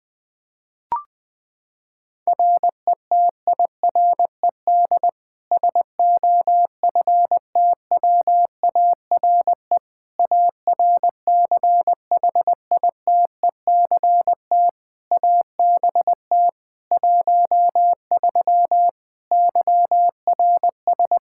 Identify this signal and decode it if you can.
Morse code, heard as a keyed tone, about 21 wpm, 700 Hz. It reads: RETIRED SOFTWARE ARCHITECT ABT 13 YRS
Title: Practice QSO - Exchange 3 - More Info - 20wpm